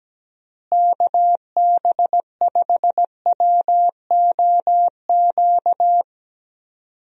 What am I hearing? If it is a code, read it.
Morse code, heard as a keyed tone, about 17 wpm, 700 Hz. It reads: KB5WOQ
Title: USA Call Signs - 6 Characters - Repeated 3x - 17wpm